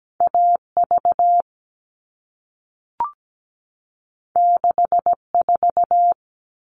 Morse code, heard as a keyed tone, about 17 wpm, 700 Hz. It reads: AV 64
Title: All Cut Numbers - 2-Digit 17wpm